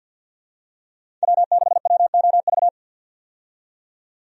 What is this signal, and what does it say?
Morse code, heard as a keyed tone, about 50 wpm, 700 Hz. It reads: W6WOV
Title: USA Call Signs - 5 Characters - 50wpm